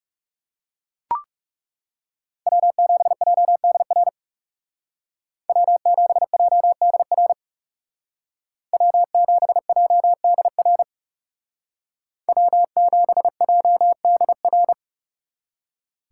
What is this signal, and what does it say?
Morse code, heard as a keyed tone, about 37 wpm, 700 Hz. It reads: W7JDR W7JDR W7JDR W7JDR
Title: Speed Racing - USA 1x3 Call Signs 30wpm